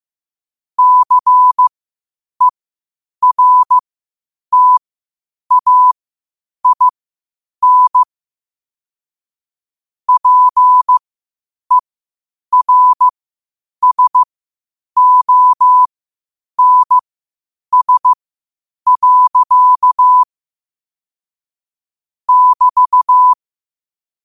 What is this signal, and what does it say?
Morse code, heard as a keyed tone, about 15 wpm, spaced as for 8 wpm, 1000 Hz. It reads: CERTAIN PERSONS. <BT>
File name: qotdfiles / 8wpm